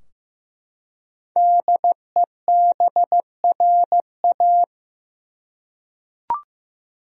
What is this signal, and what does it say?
Morse code, heard as a keyed tone, about 15 wpm, 700 Hz. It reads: DEBRA
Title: USA Common Names - 15wpm